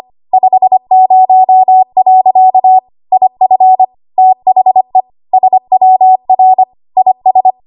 Morse code, heard as a keyed tone, about 25 wpm, 750 Hz. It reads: 50. IF THE SWR IS
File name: Archive / 25 WPM / 25WPM